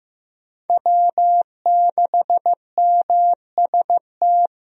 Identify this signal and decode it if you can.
Morse code, heard as a keyed tone, about 15 wpm, 700 Hz. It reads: W6MST
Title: USA Call Signs - 5 Characters - Repeated 3x - 15wpm